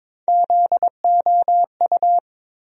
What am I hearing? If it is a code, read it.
Morse code, heard as a keyed tone, about 22 wpm, 700 Hz. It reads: ZOU
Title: USA Call Sign Postfixes - 3 Letters 22wpm